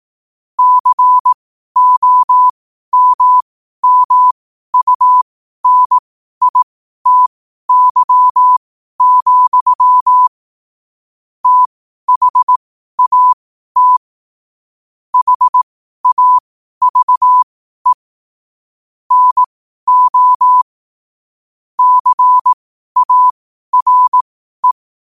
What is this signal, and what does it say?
Morse code, heard as a keyed tone, about 18 wpm, spaced as for 12 wpm, 1000 Hz. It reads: COMMUNITY, THAT HAVE NO CARE
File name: 12wpm